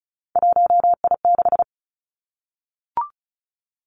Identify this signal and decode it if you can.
Morse code, heard as a keyed tone, about 36 wpm, 700 Hz. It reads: 1I6
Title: Three Challenging Letter-Number Combinations 35wpm